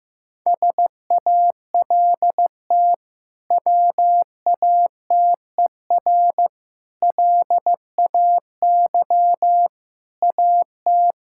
Morse code, heard as a keyed tone, about 15 wpm, 700 Hz. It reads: SALT WATER LAY AT
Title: Sentences from Top 1000 Words - Review Part 5 - 15wpm